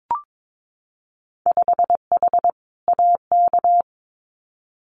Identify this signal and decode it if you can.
Morse code, heard as a keyed tone, about 22 wpm, 700 Hz. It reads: 5H AK
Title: Winter Field Day Exchanges Encore 22wpm